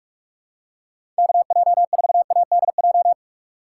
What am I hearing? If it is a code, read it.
Morse code, heard as a keyed tone, about 47 wpm, 700 Hz. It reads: KJ4ADJ
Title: USA Call Signs - 6 Characters - 4x Character Spacing - 45wpm